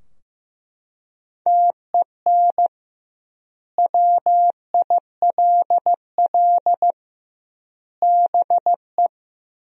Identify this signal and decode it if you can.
Morse code, heard as a keyed tone, about 15 wpm, 700 Hz. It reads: TEN WILL BE
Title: 2x Word-Spacing - Sentences from Top 400 Words - 15wpm